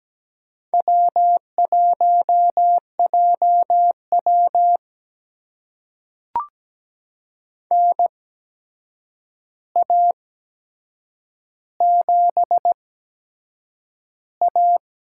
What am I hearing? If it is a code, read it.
Morse code, heard as a keyed tone, about 17 wpm, spaced as for 7 wpm, 700 Hz. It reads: W1JW NA7A
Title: USA Call Signs - 4 Characters - 8x Character Spacing - 17wpm